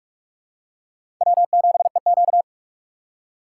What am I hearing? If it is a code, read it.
Morse code, heard as a keyed tone, about 46 wpm, 700 Hz. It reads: W7EQ
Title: USA Call Signs - 3x Character Spacing - 45wpm